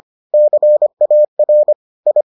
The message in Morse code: CAR I

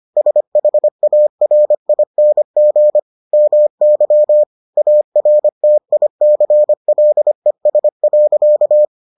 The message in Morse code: SHARING MY ARTICLES.

